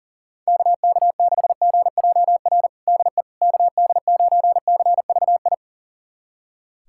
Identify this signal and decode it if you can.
Morse code, heard as a keyed tone, about 40 wpm, 700 Hz. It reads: KK6GJR DE KD9CVI
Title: USA Callsign DE Callsign - 6 Characters - 40wpm